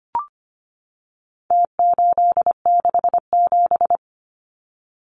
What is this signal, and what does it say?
Morse code, heard as a keyed tone, about 25 wpm, 700 Hz. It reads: T867
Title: Numbers - Contest Style - 4-Digit 25wpm